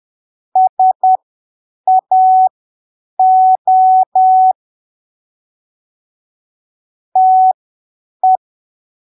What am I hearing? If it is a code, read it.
Morse code, heard as a keyed tone, about 10 wpm, spaced as for 7 wpm, 750 Hz. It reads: SAO TE
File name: arquivos / texto / texto 26 10ppm